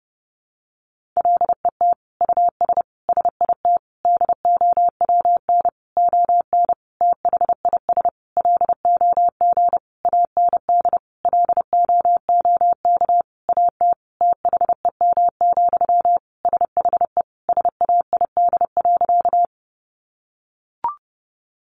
Morse code, heard as a keyed tone, about 30 wpm, 700 Hz. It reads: LET US SIT DOWN ON THIS LOG AND LOOK AT THEM, SHE SAID.
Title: Sentences from Top 1400 Words 30wpm